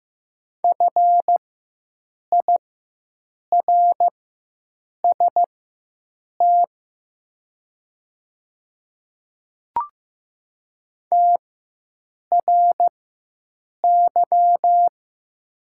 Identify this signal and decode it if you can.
Morse code, heard as a keyed tone, about 15 wpm, spaced as for 7 wpm, 700 Hz. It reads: FIRST TRY
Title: 4x Character Spacing - Top 200 Words - 15wpm